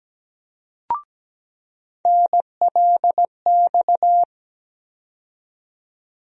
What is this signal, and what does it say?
Morse code, heard as a keyed tone, about 17 wpm, 700 Hz. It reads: NLX
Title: USA Call Sign Postfixes - 3 Letters 17wpm